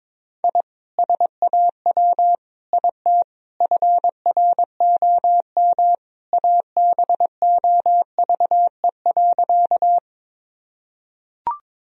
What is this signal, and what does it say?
Morse code, heard as a keyed tone, about 22 wpm, 700 Hz. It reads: I SAW IT FROM ABOVE.
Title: Sentences from Top 1000 Words - Review Part 2 - 22wpm